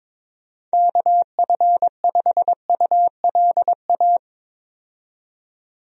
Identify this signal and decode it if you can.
Morse code, heard as a keyed tone, about 22 wpm, 700 Hz. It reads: KF5ULA
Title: USA Call Signs - 6 Characters - 22wpm